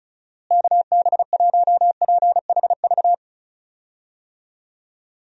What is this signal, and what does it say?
Morse code, heard as a keyed tone, about 36 wpm, 700 Hz. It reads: KB1PHV
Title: USA Call Signs - 6 Characters - Repeated 2x - 35wpm